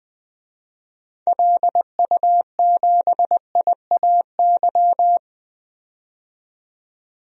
Morse code, heard as a keyed tone, about 20 wpm, 700 Hz. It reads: LU7IAY
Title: DXCC-100 Call Signs - 6 Characters - 20wpm